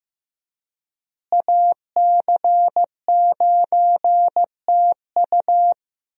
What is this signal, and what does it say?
Morse code, heard as a keyed tone, about 15 wpm, 700 Hz. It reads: AC9TU
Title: USA Call Signs - 5 Characters - Repeated 2x - 15wpm